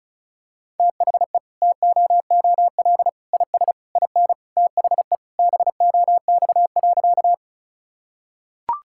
Morse code, heard as a keyed tone, about 34 wpm, 700 Hz. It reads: THE TOOL IS IN THE BOX.
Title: Sentences from Top 1000 Words - Review Part 6 - 35wpm